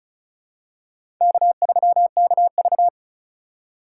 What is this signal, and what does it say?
Morse code, heard as a keyed tone, about 36 wpm, 700 Hz. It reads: K3KV
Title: USA Call Signs - 4 Characters - 4x Character Spacing - 35wpm